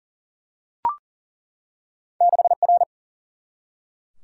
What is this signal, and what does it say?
Morse code, heard as a keyed tone, about 42 wpm, 700 Hz. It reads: 6 R
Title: Single-Letter Single-Number 40wpm